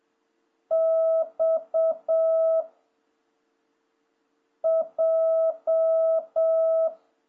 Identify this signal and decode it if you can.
Morse code, heard as a keyed tone, about 7 wpm, 660 Hz. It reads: X J
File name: jh8cbh / 2026cw